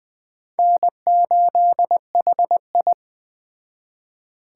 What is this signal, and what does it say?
Morse code, heard as a keyed tone, about 20 wpm, 700 Hz. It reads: N8HI